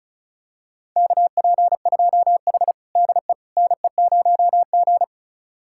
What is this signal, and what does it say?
Morse code, heard as a keyed tone, about 36 wpm, 700 Hz. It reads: KP2H DE NE0G
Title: USA Callsign DE Callsign - 4 Characters - 35wpm